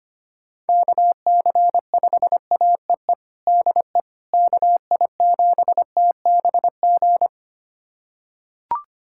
Morse code, heard as a keyed tone, about 25 wpm, 700 Hz. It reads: KC5AEE DE KI7TBG
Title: USA Callsign DE Callsign - 6 Characters - 25wpm